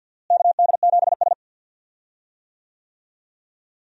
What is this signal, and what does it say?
Morse code, heard as a keyed tone, about 50 wpm, 700 Hz. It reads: KD7S
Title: USA Call Signs - 4 Characters - Repeated 3x - 50wpm